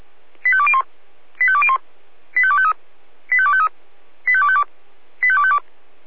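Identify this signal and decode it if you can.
Five-tone selective call (ZVEI), heard as quick runs of five tones, tones of about 70 ms. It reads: 85201 85201 85253 85253 85252 85252
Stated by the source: ZVEI five-tone call 85102 received by SDR.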